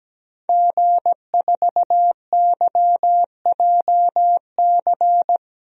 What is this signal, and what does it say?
Morse code, heard as a keyed tone, about 17 wpm, 700 Hz. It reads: G4YJC
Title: DXCC-100 Call Signs - 5 Characters - 17wpm